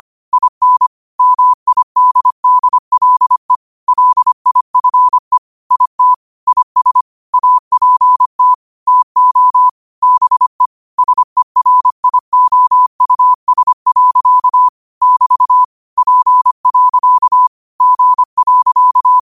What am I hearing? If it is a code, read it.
Morse code, heard as a keyed tone, about 25 wpm, 1000 Hz. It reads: IN MIDDLE LIFE IT IS APT TO BE SERIOUS. <BT> P. G.
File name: qotdfiles / 25wpm